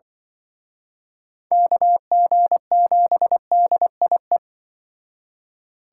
Morse code, heard as a keyed tone, about 25 wpm, 700 Hz. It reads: KG7DIE
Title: Speed Racing - USA 2x3 Call Signs 20wpm